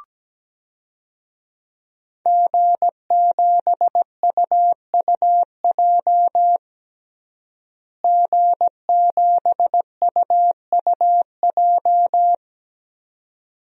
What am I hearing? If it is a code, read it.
Morse code, heard as a keyed tone, about 17 wpm, 700 Hz. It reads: G7UUJ G7UUJ
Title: DXCC-100 Call Signs - 5 Characters - Repeated 2x - 17wpm